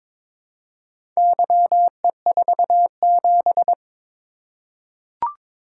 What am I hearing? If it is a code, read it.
Morse code, heard as a keyed tone, about 22 wpm, 700 Hz. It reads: YE47
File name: practice / 22wpm / Four Letter-Number Combinations - 22wpm